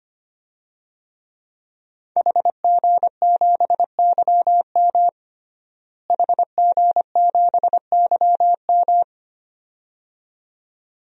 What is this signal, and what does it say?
Morse code, heard as a keyed tone, about 25 wpm, 700 Hz. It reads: HG7YM HG7YM
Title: DXCC-100 Call Signs - 5 Characters - Repeated 2x - 25wpm